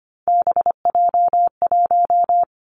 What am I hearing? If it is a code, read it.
Morse code, heard as a keyed tone, about 25 wpm, 700 Hz. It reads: BJ1
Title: Three Challenging Letter-Number Combinations 25wpm